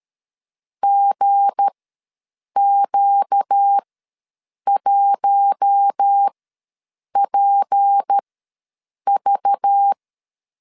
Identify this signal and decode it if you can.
Morse code, heard as a keyed tone, about 13 wpm, spaced as for 7 wpm, 800 Hz. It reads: GQ1PV
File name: LEZIONI / 7 GF9Z / 7 G ws 12wpm 3140